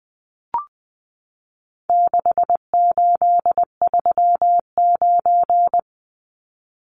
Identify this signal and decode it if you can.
Morse code, heard as a keyed tone, about 20 wpm, 700 Hz. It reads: 6839
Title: Numbers - 4-Digit 20wpm